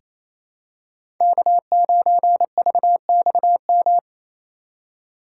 Morse code, heard as a keyed tone, about 28 wpm, 700 Hz. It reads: K9VXM
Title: USA Call Signs - 8x Character Spacing - 28wpm